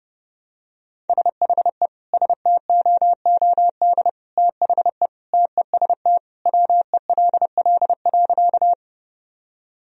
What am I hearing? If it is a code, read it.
Morse code, heard as a keyed tone, about 32 wpm, 700 Hz. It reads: SHE STOOD THE TEST WELL.
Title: Sentences from Top 1000 Words - Review Part 3 - 30wpm